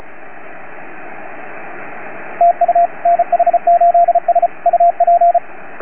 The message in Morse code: TU NH8S UP